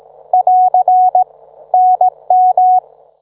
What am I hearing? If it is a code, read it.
Morse code, heard as a keyed tone, about 18 wpm, 740 Hz. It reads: <AR> NM